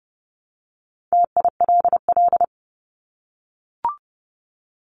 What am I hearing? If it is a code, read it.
Morse code, heard as a keyed tone, about 30 wpm, 700 Hz. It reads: TILL
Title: Top 1751-2000 Words 30wpm